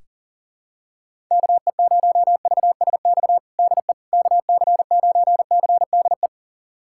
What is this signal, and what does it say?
Morse code, heard as a keyed tone, about 40 wpm, 700 Hz. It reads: KE0VSX DE KC9CDE
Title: USA Callsign DE Callsign 40wpm